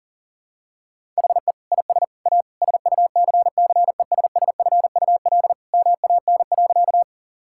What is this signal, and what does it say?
Morse code, heard as a keyed tone, about 40 wpm, 700 Hz. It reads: HE IS A SUCCESSFUL MAN.
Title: Sentences from Top 1000 Words - Review Part 6 - 40wpm